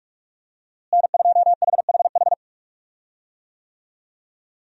Morse code, heard as a keyed tone, about 46 wpm, 700 Hz. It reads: N2HHH
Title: USA Call Signs - 5 Characters - Repeated 2x - 45wpm